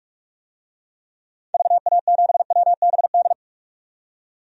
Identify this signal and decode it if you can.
Morse code, heard as a keyed tone, about 46 wpm, 700 Hz. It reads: VA7WBD
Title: DXCC-100 Call Signs - 6 Characters - 45wpm